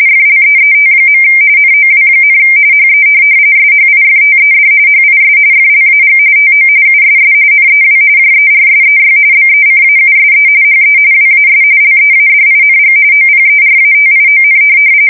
RTTY, about 45.45 baud. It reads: FLEET. NOW, HE SHIFTED HIS FLAG TO USS NEW JERSEY BECAUSE HE WAS LEADING OPERATION
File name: fsk170 / USS-New-Jersey